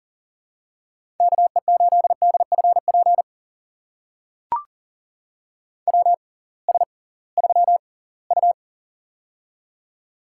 Morse code, heard as a keyed tone, about 38 wpm, spaced as for 23 wpm, 700 Hz. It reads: KE8DFP WS3U
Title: USA Call Signs - 6x Character Spacing - 40wpm